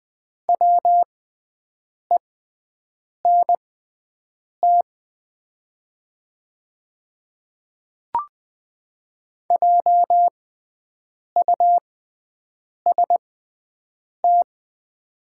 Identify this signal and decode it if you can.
Morse code, heard as a keyed tone, about 20 wpm, spaced as for 7 wpm, 700 Hz. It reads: WENT JUST
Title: 6x Character Spacing - Top 300 Words - 20wpm